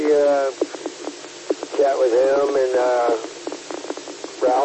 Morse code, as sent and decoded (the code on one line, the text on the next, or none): none